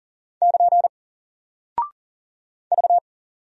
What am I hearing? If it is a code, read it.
Morse code, heard as a keyed tone, about 40 wpm, 700 Hz. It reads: <KN> V
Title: Single Character (Letter:Number:Punctuation:Prosign) Alt 40wpm